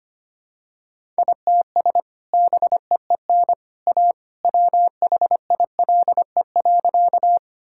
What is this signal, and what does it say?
Morse code, heard as a keyed tone, about 25 wpm, 700 Hz. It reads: ITS BEEN A WHILE.
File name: practice / 25wpm / Sentences with 2-Word Phrases - 25wpm